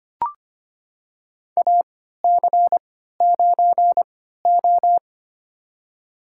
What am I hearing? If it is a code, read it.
Morse code, heard as a keyed tone, about 25 wpm, spaced as for 14 wpm, 700 Hz. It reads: AC9O
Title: USA Call Signs - 4 Characters - 3x Character Spacing - 25wpm